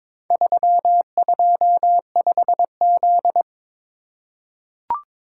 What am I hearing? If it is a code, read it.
Morse code, heard as a keyed tone, about 22 wpm, 700 Hz. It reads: 325Z
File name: Four Letter-Number Combinations - 22wpm